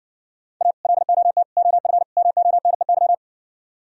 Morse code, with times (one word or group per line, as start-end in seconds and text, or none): 0.6-0.7 A
0.8-1.4 LOT
1.6-2.0 OF
2.2-3.1 MONEY